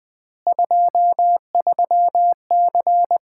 Morse code, heard as a keyed tone, about 21 wpm, 700 Hz. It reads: 23C
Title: Common QSO Phrases - 3x Word-Spacing - 20wpm